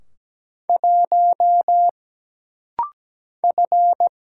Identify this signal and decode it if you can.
Morse code, heard as a keyed tone, about 17 wpm, 700 Hz. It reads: 1 F
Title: Single Letter-Number Alt - Rapid-Fire 17wpm